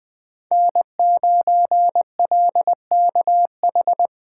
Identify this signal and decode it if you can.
Morse code, heard as a keyed tone, about 20 wpm, 700 Hz. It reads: N9LKH
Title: USA Call Signs - 5 Characters - 4x Character Spacing - 20wpm